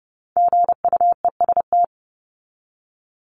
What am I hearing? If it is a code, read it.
Morse code, heard as a keyed tone, about 30 wpm, 700 Hz. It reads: GUEST
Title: Top 1501-1750 Words 30wpm